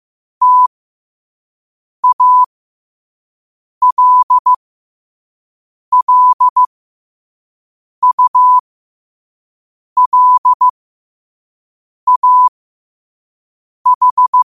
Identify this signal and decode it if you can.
Morse code, heard as a keyed tone, about 15 wpm, spaced as for 5 wpm, 1000 Hz. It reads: TALLULAH